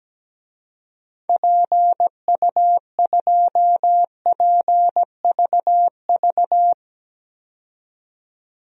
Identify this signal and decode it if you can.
Morse code, heard as a keyed tone, about 17 wpm, 700 Hz. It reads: PU2PVV
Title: DXCC-100 Call Signs - 6 Characters - 17wpm